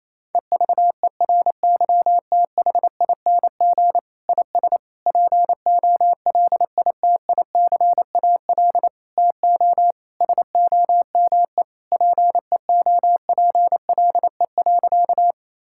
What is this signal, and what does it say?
Morse code, heard as a keyed tone, about 28 wpm, 700 Hz. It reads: EVERYTHING IS POLITICAL TO SOME PEOPLE.
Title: Sentences from Top 1000 Words - Review Part 5 - 28wpm